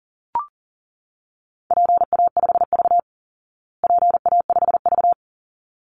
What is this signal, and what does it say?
Morse code, heard as a keyed tone, about 40 wpm, 700 Hz. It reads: PA5V PA5V
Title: Call Signs Repeated 3x - 40wpm